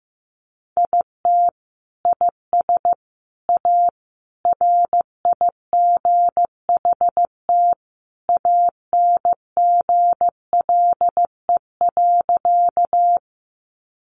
Morse code, heard as a keyed tone, about 15 wpm, 700 Hz. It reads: IT IS A RIGHT ANGLE.
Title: Sentences from Top 1500 Words 15wpm